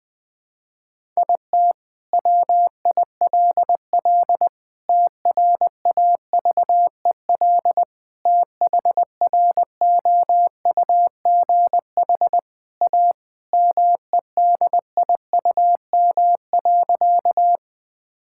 Morse code, heard as a keyed tone, about 21 wpm, 700 Hz. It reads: IT WILL TRAVEL THROUGH A MEDIUM.
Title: Sentences from Top 1000 Words - Review Part 6 - 20wpm